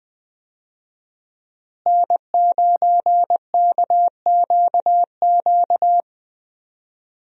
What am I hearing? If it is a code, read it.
Morse code, heard as a keyed tone, about 20 wpm, 700 Hz. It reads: N9KQQ
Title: USA Call Signs - 5 Characters - Repeated 2x - 20wpm